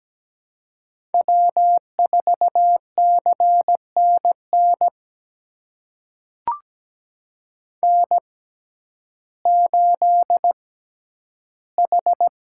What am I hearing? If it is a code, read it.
Morse code, heard as a keyed tone, about 17 wpm, spaced as for 10 wpm, 700 Hz. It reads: W4CNN N8H
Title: USA Call Signs - 5 Characters - 6x Character Spacing - 17wpm